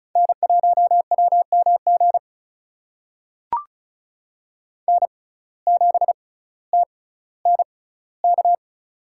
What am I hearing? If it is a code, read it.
Morse code, heard as a keyed tone, about 36 wpm, spaced as for 18 wpm, 700 Hz. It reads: N1WMG N7TNK
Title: USA Call Signs - 5 Characters - 6x Character Spacing - 35wpm